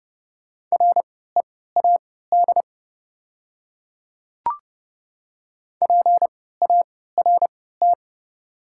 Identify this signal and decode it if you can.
Morse code, heard as a keyed tone, about 30 wpm, spaced as for 17 wpm, 690 Hz. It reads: READ PART
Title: 3x Character Spacing - Top 200 Words - 30wpm